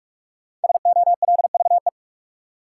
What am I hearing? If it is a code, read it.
Morse code, heard as a keyed tone, about 46 wpm, 700 Hz. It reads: SOLVE
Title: Sentences from Top 1000 Words - New Word Drill - 45wpm